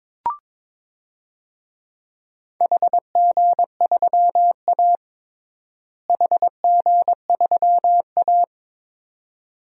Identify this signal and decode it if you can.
Morse code, heard as a keyed tone, about 22 wpm, 700 Hz. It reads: HG3A HG3A
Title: DXCC-100 Call Signs - 4 Characters - Repeated 2x - 22wpm